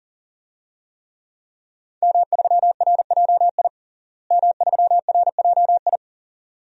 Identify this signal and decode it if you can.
Morse code, heard as a keyed tone, about 40 wpm, 700 Hz. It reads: M3RJI M3RJI
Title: DXCC-100 Call Signs - 5 Characters - Repeated 2x - 40wpm